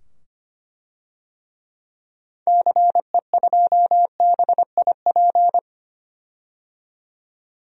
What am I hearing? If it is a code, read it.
Morse code, heard as a keyed tone, about 25 wpm, 700 Hz. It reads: CE2BIP
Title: DXCC-100 Call Signs - 6 Characters - 25wpm